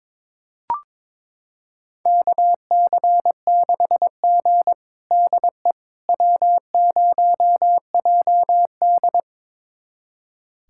Morse code, heard as a keyed tone, about 21 wpm, 700 Hz. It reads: KC6G DE W0JD
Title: USA Callsign DE Callsign - 4 Characters - 22wpm